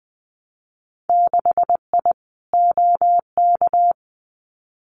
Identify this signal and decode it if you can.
Morse code, heard as a keyed tone, about 20 wpm, 700 Hz. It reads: 6I OK
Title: Winter Field Day Exchanges Encore 20wpm